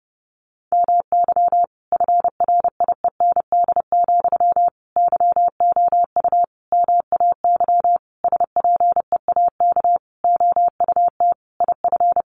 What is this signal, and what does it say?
Morse code, heard as a keyed tone, about 30 wpm, 700 Hz. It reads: MY FRIEND, YOU MAY SPEAK OUT IF
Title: Sentences from Top 1300 Words 30wpm